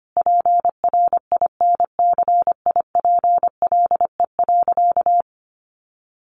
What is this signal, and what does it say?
Morse code, heard as a keyed tone, about 25 wpm, 700 Hz. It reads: PRINCIPLE.
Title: Sentences from Top 1400 Words 25wpm